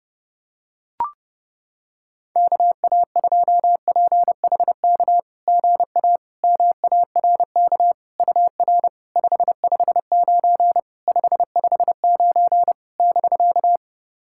Morse code, heard as a keyed tone, about 30 wpm, 690 Hz. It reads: KA2PHK GA MARK UR 559 559 <BK>
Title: SOTA QSO Practice 30wpm